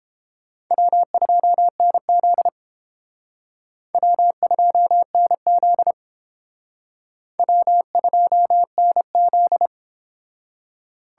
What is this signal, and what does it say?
Morse code, heard as a keyed tone, about 30 wpm, 700 Hz. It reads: W2NZ W2NZ W2NZ